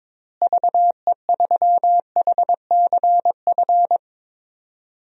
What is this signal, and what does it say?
Morse code, heard as a keyed tone, about 21 wpm, 700 Hz. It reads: VE3HCF
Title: DXCC-100 Call Signs - 6 Characters - Repeated 3x - 22wpm